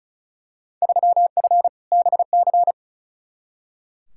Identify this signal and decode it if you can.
Morse code, heard as a keyed tone, about 34 wpm, 700 Hz. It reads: 3F BC
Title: Field Day Exchanges Encore - 35wpm